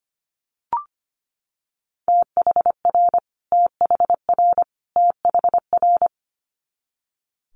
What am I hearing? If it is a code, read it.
Morse code, heard as a keyed tone, about 25 wpm, 700 Hz. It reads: THR THR THR
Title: Multi-letter Phonemes Repeated 3x 25wpm